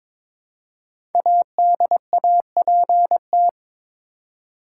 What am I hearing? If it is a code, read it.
Morse code, heard as a keyed tone, about 22 wpm, 700 Hz. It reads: ADAPT
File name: 5 Letter Words - First 3 Letters Given - 22wpm